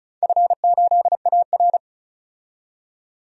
Morse code, heard as a keyed tone, about 34 wpm, 700 Hz. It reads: F8AR
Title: DXCC-100 Call Signs - 4 Characters - 35wpm